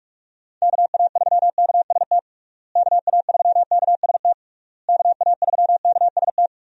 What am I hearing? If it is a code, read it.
Morse code, heard as a keyed tone, about 46 wpm, 700 Hz. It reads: KA3KST KA3KST KA3KST
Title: USA Call Signs - 6 Characters - Repeated 3x - 45wpm